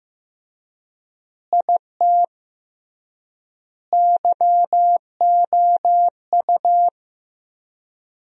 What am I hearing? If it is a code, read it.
Morse code, heard as a keyed tone, about 15 wpm, 700 Hz. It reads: IT YOU
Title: 3x Word-Spacing - Sentences from Top 200 Words - 15wpm